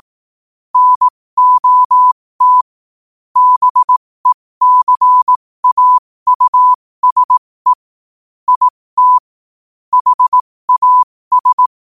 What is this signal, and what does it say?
Morse code, heard as a keyed tone, about 18 wpm, spaced as for 15 wpm, 1000 Hz. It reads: NOT BECAUSE IT HAS
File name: qotdfiles / 15wpm